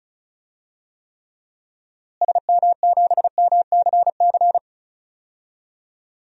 Morse code, heard as a keyed tone, about 36 wpm, 700 Hz. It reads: SM7MCC
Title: DXCC-100 Call Signs - 6 Characters - Repeated 2x - 35wpm